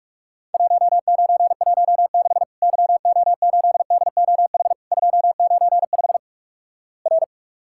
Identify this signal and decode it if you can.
Morse code, heard as a keyed tone, about 46 wpm, 700 Hz. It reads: 1916 YO8DOH 295 R
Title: Worked All Europe DX Contest - QTC Exchange - 45wpm